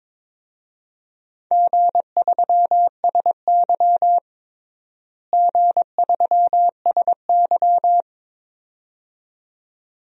Morse code, heard as a keyed tone, about 22 wpm, 700 Hz. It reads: G3SY G3SY
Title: DXCC-100 Call Signs - 4 Characters - Repeated 2x - 22wpm